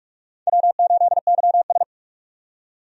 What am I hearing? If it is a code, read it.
Morse code, heard as a keyed tone, about 46 wpm, 700 Hz. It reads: W8YS
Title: USA Call Signs - 4 Characters - 45wpm